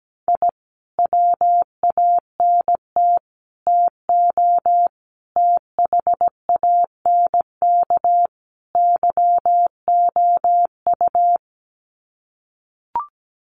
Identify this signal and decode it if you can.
Morse code, heard as a keyed tone, about 17 wpm, 700 Hz. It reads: I WANT TO THANK YOU
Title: Sets of 5 Words Encore 17wpm